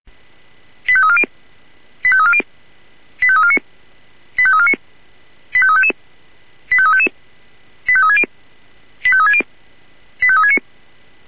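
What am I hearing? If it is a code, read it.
Five-tone selective call (ZVEI), heard as quick runs of five tones, tones of about 70 ms. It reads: 85340 85340 85349 85349 85355 85355 85360 85360 85369
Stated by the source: ZVEI five-tone selective call received by SDR.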